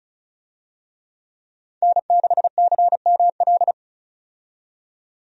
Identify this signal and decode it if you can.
Morse code, heard as a keyed tone, about 36 wpm, 700 Hz. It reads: N6CML
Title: USA Call Signs - 5 Characters - 35wpm